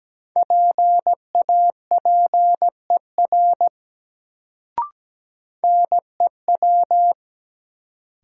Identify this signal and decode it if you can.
Morse code, heard as a keyed tone, about 17 wpm, 700 Hz. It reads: PAPER NEW
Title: Sentences from Top 300 Words - New Word Drill - 17wpm